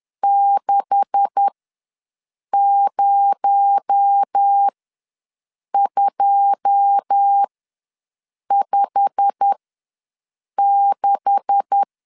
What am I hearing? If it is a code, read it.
Morse code, heard as a keyed tone, about 10 wpm, spaced as for 6 wpm, 800 Hz. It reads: 60256
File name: LEZIONI / 9 286 ws 10wpm 4032